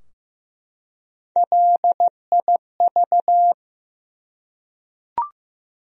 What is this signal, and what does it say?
Morse code, heard as a keyed tone, about 15 wpm, 700 Hz. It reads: LIV
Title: USA Common Names - Max 3 Letters - 15wpm